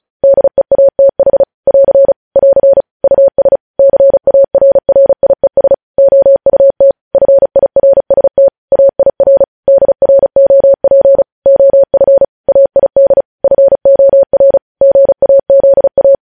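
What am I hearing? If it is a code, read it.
Morse code, heard as a keyed tone, about 36 wpm, 550 Hz. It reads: DEATH <AR> <AR> US CARRIES OUT FIRST AIR DROP OF AID FOR GAZA
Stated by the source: Morse practice at 35 wpm sending news headlines.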